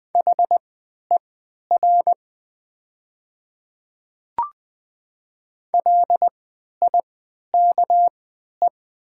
Morse code, heard as a keyed tone, about 20 wpm, spaced as for 11 wpm, 700 Hz. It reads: HER LIKE
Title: 3x Character Spacing - Top 100 Words - 20wpm